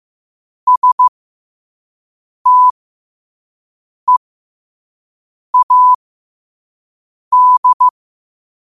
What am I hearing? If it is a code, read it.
Morse code, heard as a keyed tone, about 15 wpm, spaced as for 5 wpm, 1000 Hz. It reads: STEAD